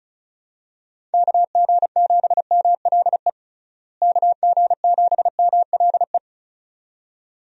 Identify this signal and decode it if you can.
Morse code, heard as a keyed tone, about 34 wpm, 700 Hz. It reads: KG7MLE KG7MLE
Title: USA Call Signs - 6 Characters - Repeated 2x - 35wpm